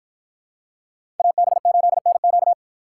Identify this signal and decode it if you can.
Morse code, heard as a keyed tone, about 50 wpm, 700 Hz. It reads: AB8NQ